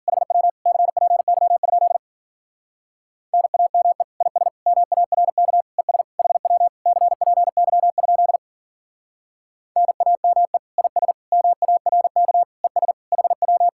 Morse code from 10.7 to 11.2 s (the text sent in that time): IS